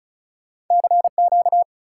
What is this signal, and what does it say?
Morse code, heard as a keyed tone, about 48 wpm, spaced as for 42 wpm, 700 Hz. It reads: CQ
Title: Speed-Racing Common QSO Phrases - 35wpm